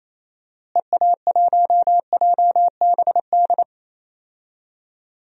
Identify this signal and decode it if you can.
Morse code, heard as a keyed tone, about 28 wpm, 700 Hz. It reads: EA1JBD
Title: DXCC-100 Call Signs - 6 Characters - Repeated 2x - 28wpm